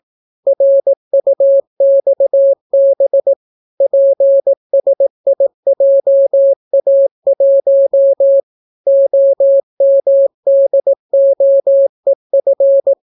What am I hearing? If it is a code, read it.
Morse code, heard as a keyed tone, about 18 wpm, 550 Hz. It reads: RUXB PSIJA1 OMDOEF